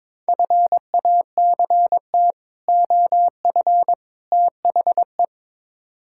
Morse code, heard as a keyed tone, about 22 wpm, 700 Hz. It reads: FACT OF THE
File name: practice / 22wpm / Build A Phrase - 5-Word Phrases - 22wpm